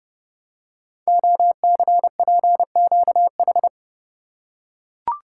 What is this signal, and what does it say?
Morse code, heard as a keyed tone, about 30 wpm, 700 Hz. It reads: OCPQH
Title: Code Groups - 5 Letter:Numbers - 30wpm